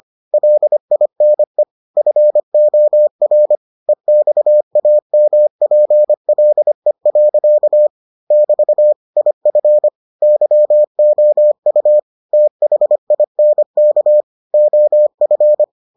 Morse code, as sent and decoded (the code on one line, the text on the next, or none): .-.. .. -. / ..-. --- .-. / . -..- .- -- .--. .-.. .-.-.- / -...- / .. ..-. / -.-- --- ..- / - .... .. -. -.- / --- ..-.
LINE FOR EXAMPLE. <BT> IF YOU THINK OF